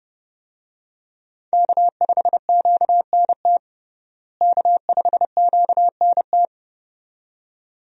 Morse code, heard as a keyed tone, about 30 wpm, 700 Hz. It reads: K5QNT K5QNT
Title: USA Call Signs - 5 Characters - Repeated 2x - 30wpm